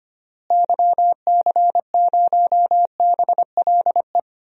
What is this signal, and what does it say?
Morse code, heard as a keyed tone, about 25 wpm, 700 Hz. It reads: YC0BLE
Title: DXCC-100 Call Signs - 6 Characters - Repeated 3x - 25wpm